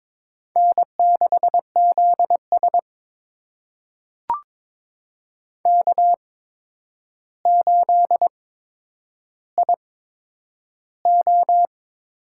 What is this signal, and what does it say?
Morse code, heard as a keyed tone, about 22 wpm, spaced as for 9 wpm, 700 Hz. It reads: N6ZS K8IO
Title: USA Call Signs - 4 Characters - 8x Character Spacing - 22wpm